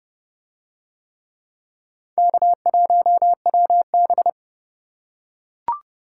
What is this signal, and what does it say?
Morse code, heard as a keyed tone, about 28 wpm, 700 Hz. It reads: K1WB